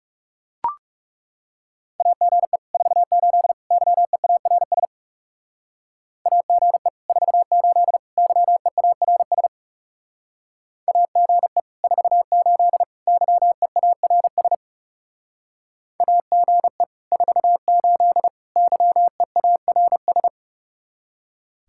Morse code, heard as a keyed tone, about 36 wpm, 700 Hz. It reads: AGE 48 YEARS AGE 48 YEARS AGE 48 YEARS AGE 48 YEARS
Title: Speed-Racing Common QSO Phrases Encore - 30wpm